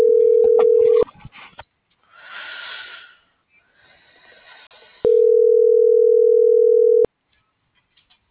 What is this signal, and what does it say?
Ringback tone (North America), 440 + 480 Hz.